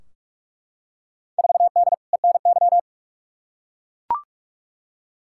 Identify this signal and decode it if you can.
Morse code, heard as a keyed tone, about 41 wpm, 700 Hz. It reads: 4D ENY
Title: Field Day Exchanges Encore - 45wpm